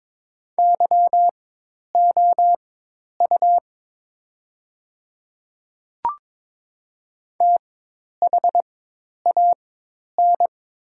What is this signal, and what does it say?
Morse code, heard as a keyed tone, about 22 wpm, spaced as for 10 wpm, 700 Hz. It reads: YOU THAN
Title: 4x Character Spacing - Top 100 Words - 22wpm